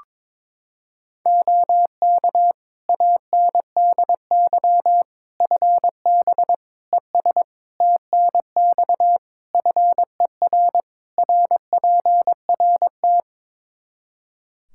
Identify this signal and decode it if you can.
Morse code, heard as a keyed tone, about 22 wpm, 700 Hz. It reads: OK ANDY FB ES TNX FER RPRT
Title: Practice QSO - Exchange 2 - Info - 22wpm